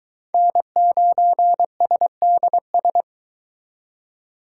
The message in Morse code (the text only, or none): N9SDS